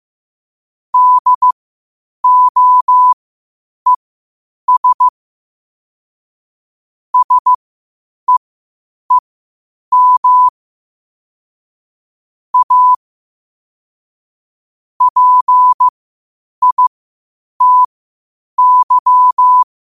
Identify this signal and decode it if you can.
Morse code, heard as a keyed tone, about 15 wpm, spaced as for 8 wpm, 1000 Hz. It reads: DOES SEEM A PITY
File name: 8wpm